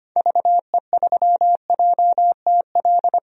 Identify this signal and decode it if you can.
Morse code, heard as a keyed tone, about 25 wpm, 700 Hz. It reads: VE3JTL
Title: DXCC-100 Call Signs - 6 Characters - 25wpm